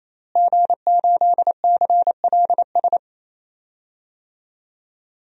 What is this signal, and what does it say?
Morse code, heard as a keyed tone, about 28 wpm, 700 Hz. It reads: G8CLS
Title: DXCC-100 Call Signs - 5 Characters - Repeated 3x - 28wpm